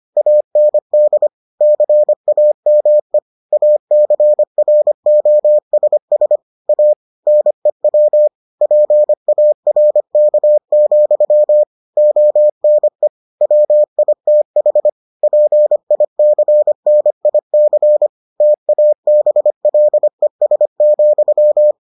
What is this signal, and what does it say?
Morse code, heard as a keyed tone, about 25 wpm, 600 Hz. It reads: AND CAME ACROSS A NEW PARK, ONE WITH PICNIC TABLES,